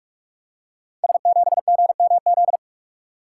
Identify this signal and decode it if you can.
Morse code, heard as a keyed tone, about 44 wpm, 700 Hz. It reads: S7GMZ
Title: Code Groups - 5 Letter:Numbers - 45wpm